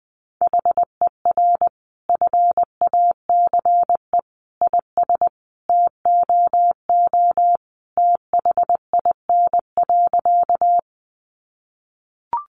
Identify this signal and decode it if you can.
Morse code, heard as a keyed tone, about 19 wpm, 700 Hz. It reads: HER FACE IS TOO THIN.
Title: Sentences from Top 1200 Words 20wpm